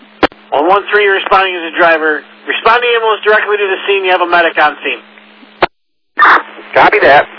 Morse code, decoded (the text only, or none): none